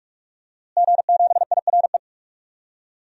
Morse code, heard as a keyed tone, about 43 wpm, 700 Hz. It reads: G7IRE
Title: DXCC-100 Call Signs - 5 Characters - Repeated 2x - 45wpm